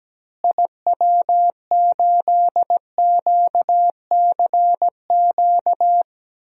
Morse code, heard as a keyed tone, about 17 wpm, 700 Hz. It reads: IW8QCQ
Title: DXCC-100 Call Signs - 6 Characters - Repeated 3x - 17wpm